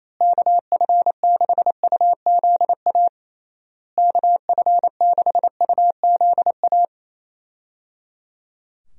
Morse code, heard as a keyed tone, about 28 wpm, 700 Hz. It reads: KF6UZA KF6UZA
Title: USA Call Signs - 6 Characters - Repeated 3x - 28wpm